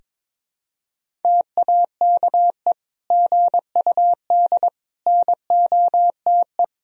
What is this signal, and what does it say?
Morse code, heard as a keyed tone, about 22 wpm, 700 Hz. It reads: TAKE GUD NOTE
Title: Sentences from Top 500 Words and Abbreviations 22wpm